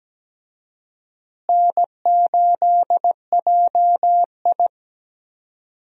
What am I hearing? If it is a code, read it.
Morse code, heard as a keyed tone, about 17 wpm, 700 Hz. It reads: N8JI
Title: USA Call Signs - 4 Characters - Repeated 3x - 17wpm